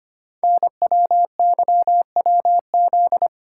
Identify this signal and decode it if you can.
Morse code, heard as a keyed tone, about 25 wpm, 700 Hz. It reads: NWYWZ
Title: Code Groups - 5 Letters - 25wpm